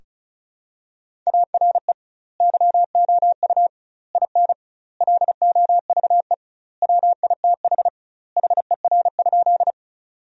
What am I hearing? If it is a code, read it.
Morse code, heard as a keyed tone, about 36 wpm, 700 Hz. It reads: ARE YOU IN LOVE WITH HER?
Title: 2x Word-Spacing - Sentences from Top 400 Words - 35wpm